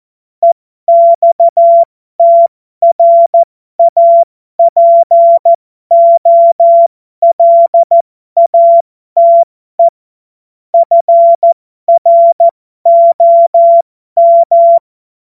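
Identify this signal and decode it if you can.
Morse code, heard as a keyed tone, about 14 wpm, spaced as for 12 wpm, 680 Hz. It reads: EXTRAPOLATE FROM
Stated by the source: Morse code joke practice.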